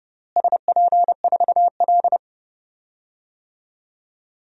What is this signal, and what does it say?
Morse code, heard as a keyed tone, about 32 wpm, 700 Hz. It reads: SP4L
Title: DXCC-100 Call Signs - 4 Characters - Repeated 3x - 30wpm